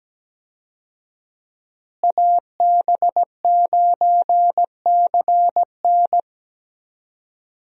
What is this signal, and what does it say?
Morse code, heard as a keyed tone, about 17 wpm, 700 Hz. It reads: AB9CN
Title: USA Call Signs - 5 Characters - Repeated 3x - 17wpm